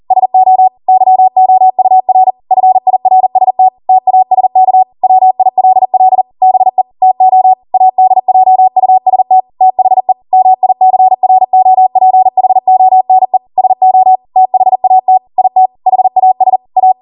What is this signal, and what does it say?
Morse code, heard as a keyed tone, about 40 wpm, 750 Hz. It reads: SO YOUR FIRST TASK WILL BE TO ADJUST THE MICROPHONE SO THAT IT HAS A